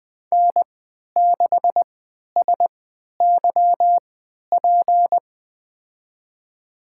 Morse code, heard as a keyed tone, about 20 wpm, spaced as for 11 wpm, 700 Hz. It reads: N6SYP
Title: USA Call Signs - 3x Character Spacing - 20wpm